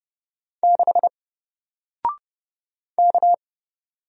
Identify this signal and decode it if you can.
Morse code, heard as a keyed tone, about 30 wpm, 700 Hz. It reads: 6 K
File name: Single-Letter-Number-30wpm